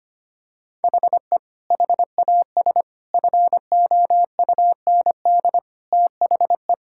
Morse code, heard as a keyed tone, about 25 wpm, 700 Hz. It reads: HE HAS FOUND THE
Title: Sentences from Top 1000 Words - Review Part 5 - 25wpm